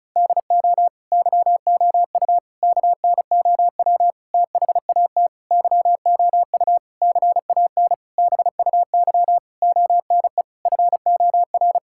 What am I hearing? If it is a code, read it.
Morse code, heard as a keyed tone, about 34 wpm, 700 Hz. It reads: DO YOU KNOW THAT YOU CAN BUY ONE FOR
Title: Sentences from Top 1000 Words - Review Part 5 - 35wpm